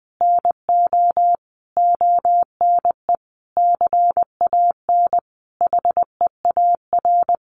Morse code, heard as a keyed tone, about 20 wpm, 700 Hz. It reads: NO ONE CAN HEAR
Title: Sentences from Top 600 Words 20wpm